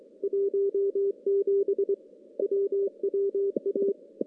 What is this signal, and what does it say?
Morse code, heard as a keyed tone, about 25 wpm, 400 Hz. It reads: 17 WWA
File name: contests / CW-SS-k5tr- -2300